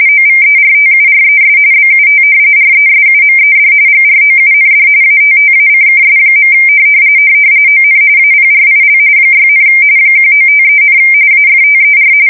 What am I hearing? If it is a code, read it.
RTTY, about 45.45 baud. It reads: OVERTURES BUT RESUMED GROUND ATTACKS. ON APRIL 6, 1953, AFTER
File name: fsk170 / battleshipnj